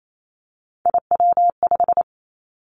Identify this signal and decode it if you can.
Morse code, heard as a keyed tone, about 28 wpm, 690 Hz. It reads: IW5
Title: Three Challenging Letter-Number Combinations 28wpm